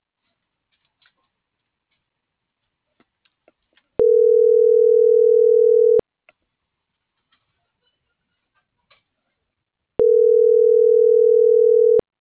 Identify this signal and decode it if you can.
Ringback tone (North America), 440 + 480 Hz.